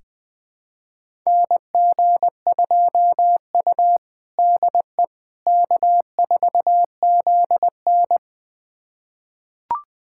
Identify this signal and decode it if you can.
Morse code, heard as a keyed tone, about 20 wpm, 700 Hz. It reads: NG2U DE K4ZN
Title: USA Callsign DE Callsign - 4 Characters - 20wpm